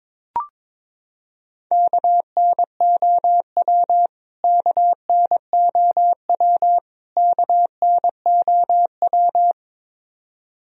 Morse code, heard as a keyed tone, about 22 wpm, 700 Hz. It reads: KNOW KNOW KNOW
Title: Top 100 Words Repeated 22wpm